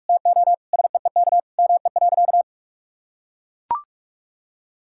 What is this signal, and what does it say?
Morse code, heard as a keyed tone, about 41 wpm, 700 Hz. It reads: TO SEEK ME.
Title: Sentences from Top 1600 Words 45wpm